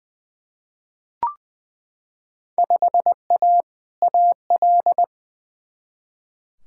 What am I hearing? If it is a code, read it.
Morse code, heard as a keyed tone, about 21 wpm, 700 Hz. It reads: 5A AL
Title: Field Day Exchanges Encore - 20wpm